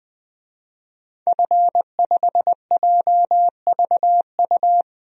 Morse code, heard as a keyed tone, about 20 wpm, 700 Hz. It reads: F5JVU
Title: DXCC-100 Call Signs - 5 Characters - 20wpm